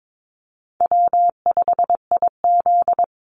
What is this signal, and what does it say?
Morse code, heard as a keyed tone, about 22 wpm, 700 Hz. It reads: W5IZ
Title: Easy Call Signs Repeated 3x - 22wpm